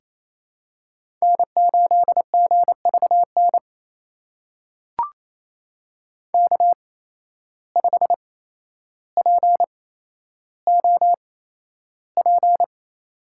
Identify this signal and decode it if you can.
Morse code, heard as a keyed tone, about 28 wpm, spaced as for 12 wpm, 700 Hz. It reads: N8GVN K5POP
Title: USA Call Signs - 5 Characters - 8x Character Spacing - 28wpm